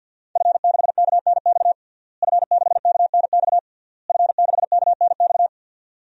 Morse code, heard as a keyed tone, about 52 wpm, 700 Hz. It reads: F6KNX F6KNX F6KNX
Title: DXCC-100 Call Signs - 5 Characters - Repeated 3x - 50wpm